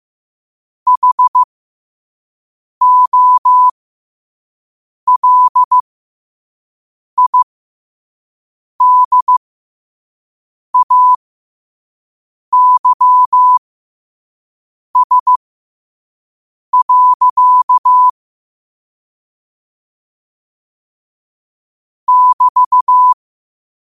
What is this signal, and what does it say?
Morse code, heard as a keyed tone, about 15 wpm, spaced as for 5 wpm, 1000 Hz. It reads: HOLIDAYS. <BT>
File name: qotdfiles / 5wpm